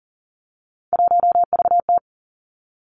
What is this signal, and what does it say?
Morse code, heard as a keyed tone, about 38 wpm, 700 Hz. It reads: 1VT
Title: Three Challenging Letter-Number Combinations 40wpm